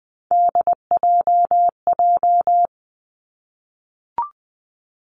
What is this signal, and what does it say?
Morse code, heard as a keyed tone, about 20 wpm, 700 Hz. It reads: DJJ
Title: Three Challenging Letter-Number Combinations 20wpm